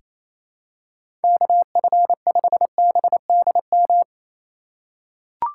Morse code, heard as a keyed tone, about 28 wpm, 700 Hz. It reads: KF5BDM
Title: USA Call Signs - 6 Characters - 2x Character Spacing - 28wpm